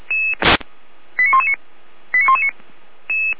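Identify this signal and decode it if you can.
Five-tone selective call (ZVEI), heard as quick runs of five tones, tones of about 70 ms. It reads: 89119 89119
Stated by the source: ZVEI five-tone selective call received on an SDR.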